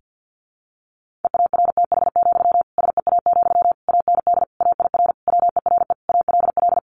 Morse code, heard as a keyed tone, about 50 wpm, 700 Hz. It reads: EARTH, SEA, AND AIR WERE ALL
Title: Sentences from Top 1500 Words 50wpm